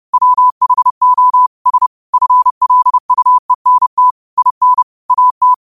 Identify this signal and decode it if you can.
Morse code, heard as a keyed tone, about 30 wpm, 1000 Hz. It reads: WHO S FLUENT IN AT
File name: qotdfiles / 30wpm